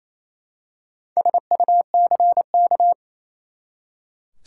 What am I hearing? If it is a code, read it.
Morse code, heard as a keyed tone, about 28 wpm, 700 Hz. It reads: SUCK
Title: Four Letter Words - 28wpm